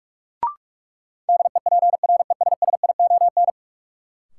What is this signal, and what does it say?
Morse code, heard as a keyed tone, about 46 wpm, 700 Hz. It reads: DEPRESSION
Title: Sentences from Top 1300 Words - New Word Drill - 45wpm